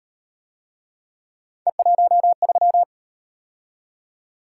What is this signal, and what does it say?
Morse code, heard as a keyed tone, about 37 wpm, 700 Hz. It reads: E13